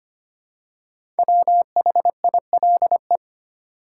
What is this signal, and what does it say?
Morse code, heard as a keyed tone, about 25 wpm, 700 Hz. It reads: WHILE
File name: 5 Letter Words - First 3 Letters Given - 25wpm